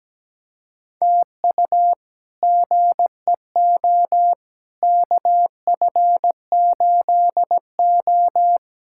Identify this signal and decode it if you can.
Morse code, heard as a keyed tone, about 17 wpm, 700 Hz. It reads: TU GEO KF8O
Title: K1USN SST QSO Practice - 17wpm